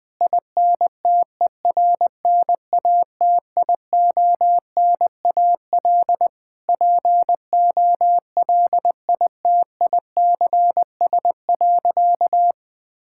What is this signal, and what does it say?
Morse code, heard as a keyed tone, about 20 wpm, 700 Hz. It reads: INTERNATIONAL POLITICS.
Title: Sentences from Top 1000 Words - Review Part 6 - 20wpm